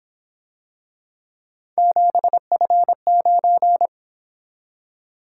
Morse code, heard as a keyed tone, about 26 wpm, 700 Hz. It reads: 7F9
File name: Speed-Racing - Three Letter-Number Combinations - 22wpm